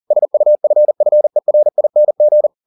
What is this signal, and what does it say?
Morse code, heard as a keyed tone, about 40 wpm, 600 Hz. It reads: SUFFERING